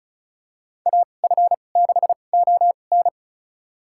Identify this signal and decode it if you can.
Morse code, heard as a keyed tone, about 36 wpm, spaced as for 25 wpm, 700 Hz. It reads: AF6ON
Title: USA Call Signs - 2x Character Spacing - 35wpm